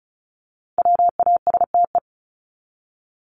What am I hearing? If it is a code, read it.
Morse code, heard as a keyed tone, about 34 wpm, 700 Hz. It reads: WASTE
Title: Top 1751-2000 Words 35wpm